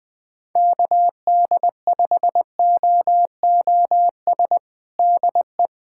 Morse code, heard as a keyed tone, about 20 wpm, 700 Hz. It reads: KD5OOS DE
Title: USA Callsign DE Callsign 20wpm